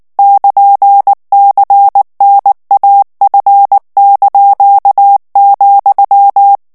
Morse code, heard as a keyed tone, about 19 wpm, 800 Hz. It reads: <KN>CNAF),
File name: morse mp3 / ESPAGNOL TEXTE EN CLAIR 2 VITESSE 18 MOTS MINUTES